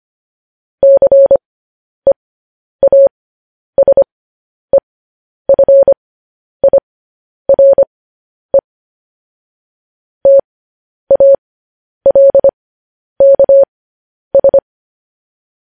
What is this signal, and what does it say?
Morse code, heard as a keyed tone, about 25 wpm, spaced as for 10 wpm, 550 Hz. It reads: CEASEFIRE TALKS
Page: News Headlines in Morse Code at 10 WPM